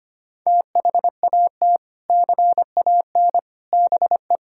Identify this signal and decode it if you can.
Morse code, heard as a keyed tone, about 25 wpm, 700 Hz. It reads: THAT CAN BE
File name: practice / 25wpm / Sentences with 3-Word Phrases - 25wpm